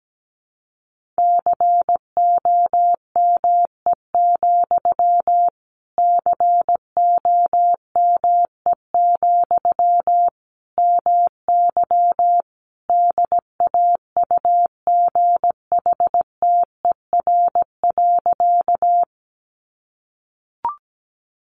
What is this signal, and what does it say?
Morse code, heard as a keyed tone, about 17 wpm, 700 Hz. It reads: COME, COME, MY DAUGHTER.
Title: Sentences from Top 1500 Words 17wpm